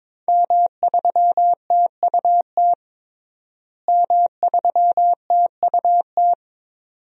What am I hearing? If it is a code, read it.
Morse code, heard as a keyed tone, about 22 wpm, 700 Hz. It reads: M3TUT M3TUT
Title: DXCC-100 Call Signs - 5 Characters - Repeated 2x - 22wpm